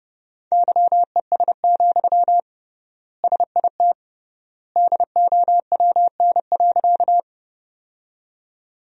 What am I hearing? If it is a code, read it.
Morse code, heard as a keyed tone, about 30 wpm, 700 Hz. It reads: YES, SIT DOWN.
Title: 3x Word-Spacing - Sentences from Top 500 Words - 30wpm